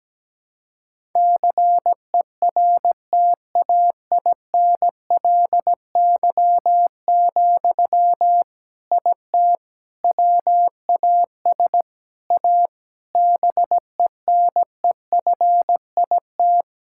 Morse code, heard as a keyed tone, about 17 wpm, 700 Hz. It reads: CERTAINLY, IT WAS A BENEFIT
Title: Sentences from Top 1000 Words - Review Part 6 - 17wpm